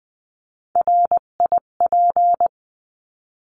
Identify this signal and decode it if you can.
Morse code, heard as a keyed tone, about 20 wpm, 700 Hz. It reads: RIP
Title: 258 Lesson 42c - Words 16-20wpm